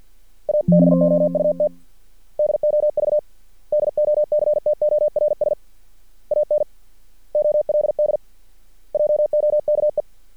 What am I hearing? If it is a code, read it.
Morse code, heard as a keyed tone, about 50 wpm, 600 Hz. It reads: ABOUT DOV DOCTORS AN OLD JOKE
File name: archived / CRSnet